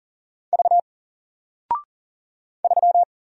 Morse code, heard as a keyed tone, about 38 wpm, 700 Hz. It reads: V 3
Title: Single Character (Letter:Number:Punctuation:Prosign) 40wpm